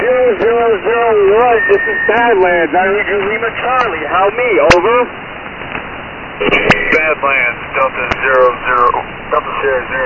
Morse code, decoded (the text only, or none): none